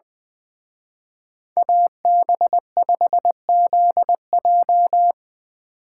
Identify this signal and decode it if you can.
Morse code, heard as a keyed tone, about 21 wpm, 700 Hz. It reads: AB5ZJ
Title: Speed Racing - USA 2x2 Call Signs 20wpm